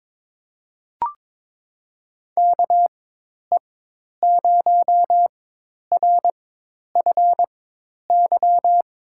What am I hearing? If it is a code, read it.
Morse code, heard as a keyed tone, about 23 wpm, spaced as for 10 wpm, 700 Hz. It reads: KE0RFY
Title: USA Call Signs - 6 Characters - 4x Character Spacing - 22wpm